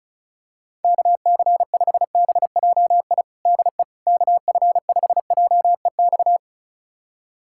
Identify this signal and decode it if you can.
Morse code, heard as a keyed tone, about 34 wpm, 700 Hz. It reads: KC5BJI DE KF5JEX
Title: USA Callsign DE Callsign - 6 Characters - 35wpm